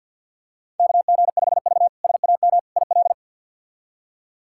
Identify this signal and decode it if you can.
Morse code, heard as a keyed tone, about 46 wpm, 700 Hz. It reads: KG5V SAM IL
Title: K1USN SST QSO Practice - 50wpm